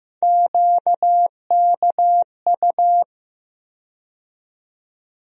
Morse code, heard as a keyed tone, about 15 wpm, 700 Hz. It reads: QKU
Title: USA Call Sign Postfixes - 3 Letters 15wpm